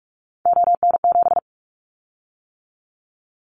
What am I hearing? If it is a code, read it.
Morse code, heard as a keyed tone, about 46 wpm, 690 Hz. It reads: ON7
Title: Callsign Prefixes for Top 50 DXCC - 45wpm